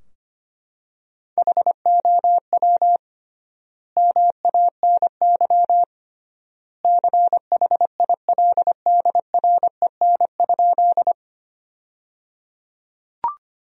Morse code, heard as a keyed tone, about 25 wpm, 700 Hz. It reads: HOW MANY CHILDREN?
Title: 3x Word-Spacing - Sentences from Top 300 Words - 25wpm